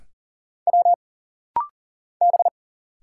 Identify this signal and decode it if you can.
Morse code, heard as a keyed tone, about 40 wpm, 700 Hz. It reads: W B
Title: Single Letters - Rapid-Fire 40wpm